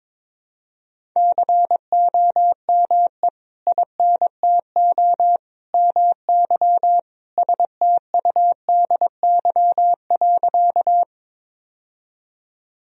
Morse code, heard as a keyed tone, about 22 wpm, 700 Hz. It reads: COME INTO MY STUDY.
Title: Sentences from Top 1000 Words - Review Part 1 - 22wpm